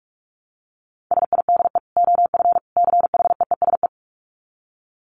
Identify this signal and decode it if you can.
Morse code, heard as a keyed tone, about 44 wpm, 700 Hz. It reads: SIDE OF CHEESE
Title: Dit Challenge 45wpm